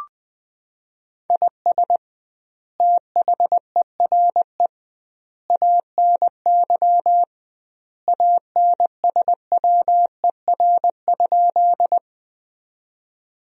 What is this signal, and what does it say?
Morse code, heard as a keyed tone, about 20 wpm, 700 Hz. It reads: IS THERE ANY ANSWER?
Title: 2x Word-Spacing - Sentences from Top 300 Words - 20wpm